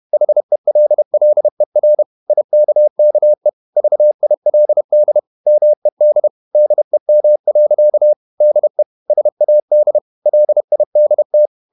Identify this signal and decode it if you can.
Morse code, heard as a keyed tone, about 31 wpm, 600 Hz. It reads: HELLER IKKE VILD MED DEM. DE SAD LIDT
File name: Ferie QRQ 31wpm